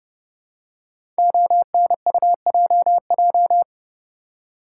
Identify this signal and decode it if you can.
Morse code, heard as a keyed tone, about 30 wpm, 700 Hz. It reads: ONUJJ
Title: Code Groups - 5 Letters - 30wpm